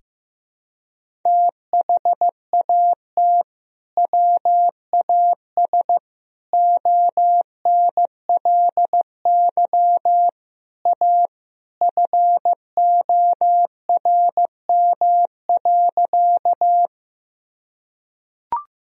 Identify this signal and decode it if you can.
Morse code, heard as a keyed tone, about 15 wpm, 700 Hz. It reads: THAT WAS ONLY A FORM.
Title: Sentences from Top 1000 Words - Review Part 1 - 15wpm